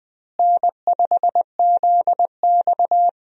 Morse code, heard as a keyed tone, about 20 wpm, 700 Hz. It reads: N5ZX
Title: USA Call Signs - 4 Characters - Repeated 3x - 20wpm